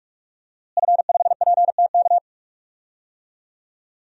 Morse code, heard as a keyed tone, about 46 wpm, 700 Hz. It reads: F5PTK
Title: DXCC-100 Call Signs - 5 Characters - Repeated 2x - 45wpm